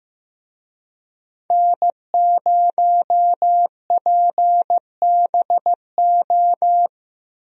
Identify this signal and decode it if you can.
Morse code, heard as a keyed tone, about 15 wpm, 700 Hz. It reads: N0PBO
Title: USA Call Signs - 5 Characters - Repeated 2x - 15wpm